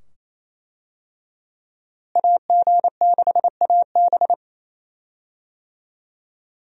Morse code, heard as a keyed tone, about 28 wpm, 700 Hz. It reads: AG6AB
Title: USA Call Signs - 5 Characters - Repeated 2x - 28wpm